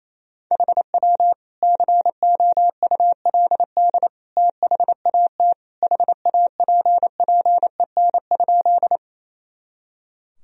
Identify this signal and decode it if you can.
Morse code, heard as a keyed tone, about 27 wpm, 700 Hz. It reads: HW COULD THAT HAPPEN?
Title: Sentences from Top 400 Words and Abbreviations 28wpm